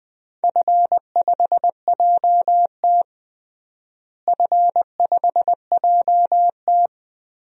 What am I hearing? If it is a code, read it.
Morse code, heard as a keyed tone, about 20 wpm, 700 Hz. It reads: F5JT F5JT
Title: DXCC-100 Call Signs - 4 Characters - Repeated 3x - 20wpm